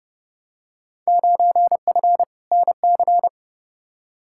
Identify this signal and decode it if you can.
Morse code, heard as a keyed tone, about 30 wpm, 700 Hz. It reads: 9F NC
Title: Field Day Exchanges Encore - 30wpm